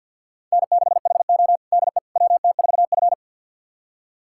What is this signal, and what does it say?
Morse code, heard as a keyed tone, about 49 wpm, 700 Hz. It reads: N6HO DE WT4F
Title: USA Callsign DE Callsign - 4 Characters - 50wpm